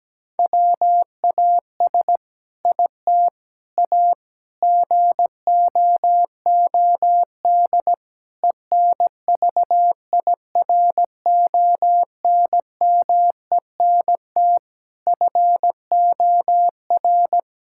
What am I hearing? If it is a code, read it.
Morse code, heard as a keyed tone, about 17 wpm, 700 Hz. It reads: WAS IT A GOOD ENVIRONMENT FOR
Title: Sentences from Top 1000 Words - Review Part 5 - 17wpm